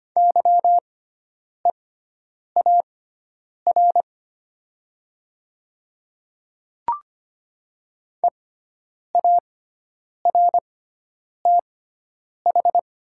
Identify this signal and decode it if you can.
Morse code, heard as a keyed tone, about 25 wpm, spaced as for 9 wpm, 700 Hz. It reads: YEAR EARTH
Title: 6x Character Spacing - Top 200 Words - 25wpm